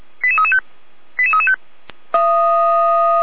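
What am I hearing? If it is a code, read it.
Five-tone selective call (ZVEI), heard as quick runs of five tones, tones of about 70 ms. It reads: 88335 88335
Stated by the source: ZVEI five-tone selective call 88139 received on an SDR.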